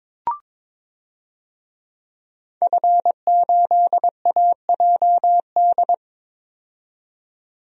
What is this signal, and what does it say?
Morse code, heard as a keyed tone, about 22 wpm, 700 Hz. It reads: F8AJD
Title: DXCC-100 Call Signs - 5 Characters - 22wpm